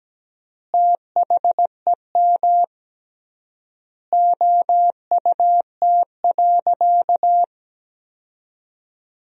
Morse code, heard as a keyed tone, about 17 wpm, 700 Hz. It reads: THEM OUT.
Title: 3x Word-Spacing - Sentences from Top 500 Words - 17wpm